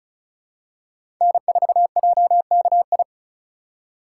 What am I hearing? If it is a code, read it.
Morse code, heard as a keyed tone, about 36 wpm, 700 Hz. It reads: N4JKI
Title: USA Call Signs - 5 Characters - 4x Character Spacing - 35wpm